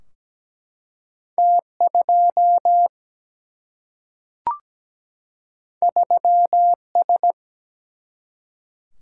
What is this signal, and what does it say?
Morse code, heard as a keyed tone, about 17 wpm, 700 Hz. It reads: T 2 3 S
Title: Single-Letter Single-Number 17wpm